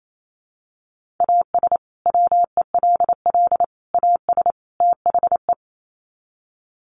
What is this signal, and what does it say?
Morse code, heard as a keyed tone, about 28 wpm, 700 Hz. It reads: AS WELL AS THE
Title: Sets of 4 Words 28wpm